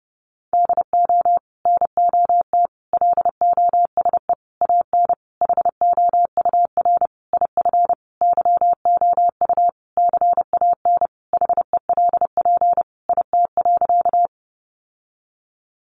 Morse code, heard as a keyed tone, about 30 wpm, 700 Hz. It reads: DO NOT LOSE AN HOUR IF YOU CAN HELP IT.
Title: Sentences from Top 1100 Words 30wpm